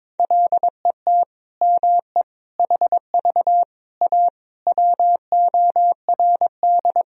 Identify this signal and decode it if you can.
Morse code, heard as a keyed tone, about 22 wpm, 700 Hz. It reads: LET ME HV A WORD
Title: Sentences from Top 300 Words and Abbreviations 22wpm